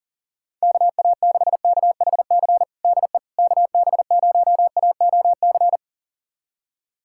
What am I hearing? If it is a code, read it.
Morse code, heard as a keyed tone, about 40 wpm, 700 Hz. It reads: KA6KHC DE KB0AOC
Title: USA Callsign DE Callsign - 6 Characters - 40wpm